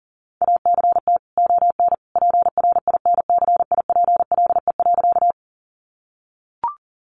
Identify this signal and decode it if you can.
Morse code, heard as a keyed tone, about 37 wpm, 700 Hz. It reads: ACT ON PRINCIPLE.
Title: Sentences from Top 1400 Words 40wpm